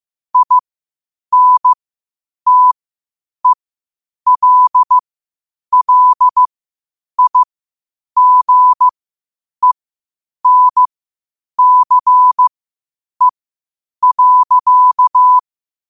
Morse code, heard as a keyed tone, about 15 wpm, spaced as for 8 wpm, 1000 Hz. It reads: INTELLIGENCE.
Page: Quotes for Fri, 15 Aug 2025 in Morse Code at 8 words per minute.